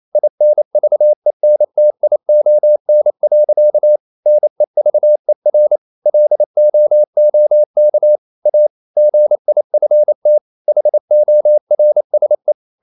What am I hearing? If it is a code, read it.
Morse code, heard as a keyed tone, about 28 wpm, 600 Hz. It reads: INVENTION. NEVER LOOK A GIFT HORSE
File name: CW / Practice / EnglishSayings28